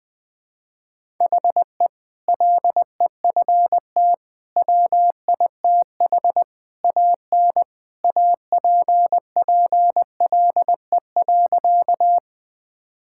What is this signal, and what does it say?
Morse code, heard as a keyed tone, about 20 wpm, 700 Hz. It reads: HE LEFT WITH AN APPLE.
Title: Sentences with 2-Word Phrases - 20wpm